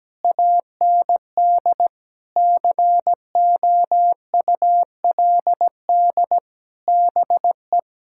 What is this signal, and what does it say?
Morse code, heard as a keyed tone, about 17 wpm, 700 Hz. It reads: AND COULD BE
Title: Sentences from Top 1000 Words - Review Part 6 - 17wpm